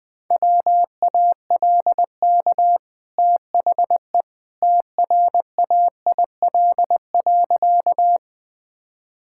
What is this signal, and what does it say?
Morse code, heard as a keyed tone, about 20 wpm, 700 Hz. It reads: WALK THE TRAIL.
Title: Easy Sentences with Hints - 20wpm